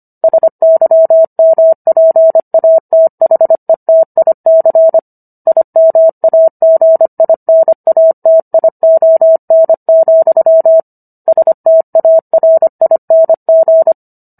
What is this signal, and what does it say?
Morse code, heard as a keyed tone, about 25 wpm, 650 Hz. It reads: SYMPATHETIC IMAGINATION, STARING